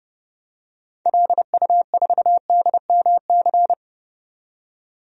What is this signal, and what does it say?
Morse code, heard as a keyed tone, about 32 wpm, 700 Hz. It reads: LU4DMC
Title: DXCC-100 Call Signs - 6 Characters - 30wpm